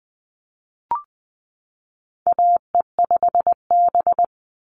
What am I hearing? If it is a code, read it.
Morse code, heard as a keyed tone, about 20 wpm, 700 Hz. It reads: AE5B
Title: NA SOTA Chaser Callsigns 20wpm